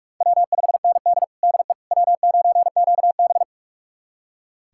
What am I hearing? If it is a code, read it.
Morse code, heard as a keyed tone, about 42 wpm, 700 Hz. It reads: W5ND DE W9QB
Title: USA Callsign DE Callsign - 4 Characters - 45wpm